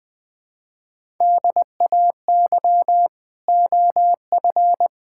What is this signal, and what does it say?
Morse code, heard as a keyed tone, about 20 wpm, 700 Hz. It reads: DAY OF
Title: Sets of 9 Words - 20wpm